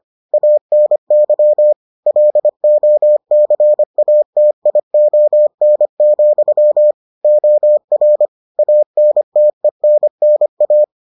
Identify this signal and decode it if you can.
Morse code, heard as a keyed tone, about 25 wpm, 600 Hz. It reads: ANY LOCATION, OR ANTENNA